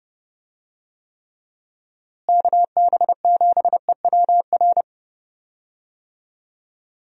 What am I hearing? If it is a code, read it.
Morse code, heard as a keyed tone, about 30 wpm, 700 Hz. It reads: KB7EWR
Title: USA Call Signs - 6 Characters - Repeated 2x - 30wpm